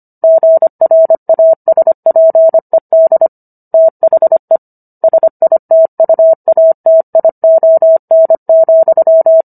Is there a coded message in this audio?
Yes: Morse code from 0.2 to 9.4 s, heard as a keyed tone, about 25 wpm, 650 Hz. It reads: GRASPED THE SITUATION,